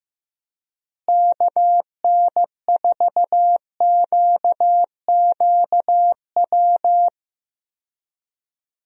Morse code, heard as a keyed tone, about 15 wpm, 700 Hz. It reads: KN4QQW
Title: USA Call Signs - 6 Characters - Repeated 2x - 15wpm